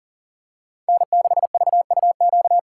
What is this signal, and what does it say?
Morse code, heard as a keyed tone, about 40 wpm, 700 Hz. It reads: N6VUQ
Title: USA Call Signs - 5 Characters - 4x Character Spacing - 40wpm